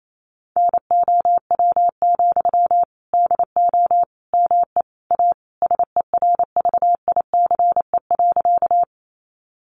Morse code, heard as a keyed tone, about 28 wpm, 700 Hz. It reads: NOW, DO ME A SERVICE.
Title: Sentences from Top 700 Words 28wpm